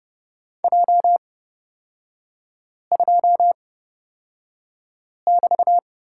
Morse code, heard as a keyed tone, about 31 wpm, spaced as for 5 wpm, 700 Hz. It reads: J2<BT>
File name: Lesson 36e - BT - Three Characters 30wpm